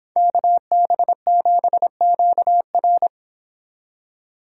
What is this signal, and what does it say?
Morse code, heard as a keyed tone, about 25 wpm, 700 Hz. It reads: KB7QR